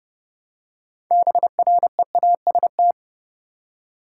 Morse code, heard as a keyed tone, about 30 wpm, 700 Hz. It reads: BREAST
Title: Six Letter Words - 30wpm